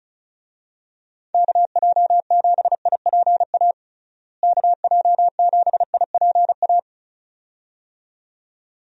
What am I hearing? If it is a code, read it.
Morse code, heard as a keyed tone, about 34 wpm, 700 Hz. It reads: KJ7IPA KJ7IPA
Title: USA Call Signs - 6 Characters - Repeated 2x - 35wpm